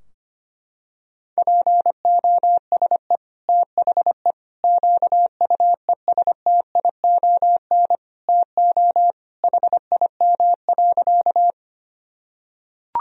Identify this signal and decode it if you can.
Morse code, heard as a keyed tone, about 25 wpm, 700 Hz. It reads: POSE THE QUESTION TO HIM.
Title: Sentences from Top 1000 Words - Review Part 2 - 25wpm